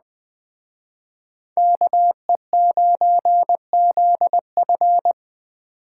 Morse code, heard as a keyed tone, about 20 wpm, 700 Hz. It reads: KE9ZF